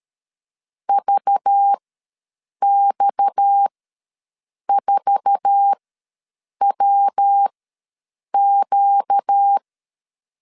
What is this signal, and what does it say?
Morse code, heard as a keyed tone, about 13 wpm, spaced as for 7 wpm, 800 Hz. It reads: VX4WQ
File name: LEZIONI / 8 YJ45 / 8 YJ4 ws 12wpm 3249